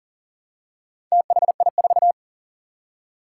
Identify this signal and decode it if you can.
Morse code, heard as a keyed tone, about 41 wpm, 700 Hz. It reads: THI4
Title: Four Challenging Letter-Number Combinations 40wpm